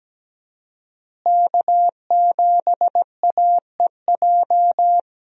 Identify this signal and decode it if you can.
Morse code, heard as a keyed tone, about 17 wpm, 700 Hz. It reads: K7AEJ
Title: USA Call Signs - 5 Characters - Repeated 2x - 17wpm